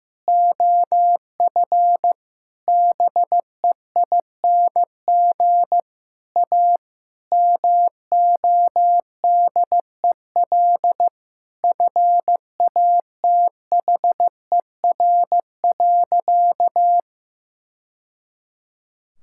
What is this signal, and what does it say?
Morse code, heard as a keyed tone, about 15 wpm, 700 Hz. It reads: OF BEING A MODEL FATHER.
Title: Sentences from Top 1000 Words - Review Part 5 - 15wpm